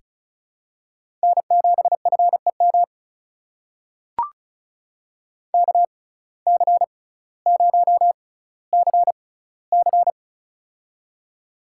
Morse code, heard as a keyed tone, about 36 wpm, spaced as for 18 wpm, 700 Hz. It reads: N7FEM KC0CC
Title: USA Call Signs - 5 Characters - 6x Character Spacing - 35wpm